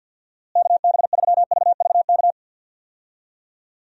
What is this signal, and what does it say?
Morse code, heard as a keyed tone, about 52 wpm, 700 Hz. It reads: KB3VVK
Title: USA Call Signs - 6 Characters - Repeated 3x - 50wpm